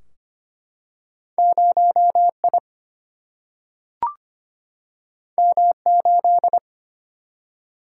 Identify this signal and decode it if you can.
Morse code, heard as a keyed tone, about 25 wpm, 700 Hz. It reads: 0 I M 8
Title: Single-Letter Single-Number 25wpm